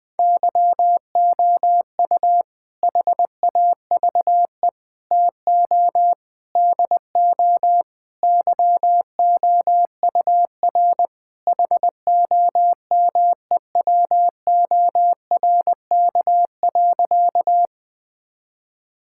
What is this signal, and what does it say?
Morse code, heard as a keyed tone, about 20 wpm, 700 Hz. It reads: YOU HAVE TO DO YOUR HOMEWORK.
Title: Sentences with 4-Word Phrases - 20wpm